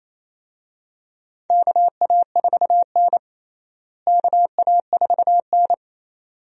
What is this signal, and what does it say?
Morse code, heard as a keyed tone, about 28 wpm, 700 Hz. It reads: KA4N KA4N
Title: USA Call Signs - 4 Characters - Repeated 2x - 28wpm